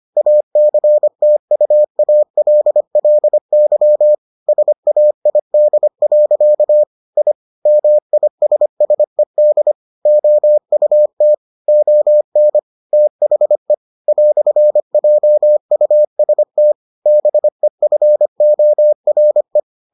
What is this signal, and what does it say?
Morse code, heard as a keyed tone, about 25 wpm, 600 Hz. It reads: ACTUALLY SAID. I MISSED OUT ON THE "JUST BEFORE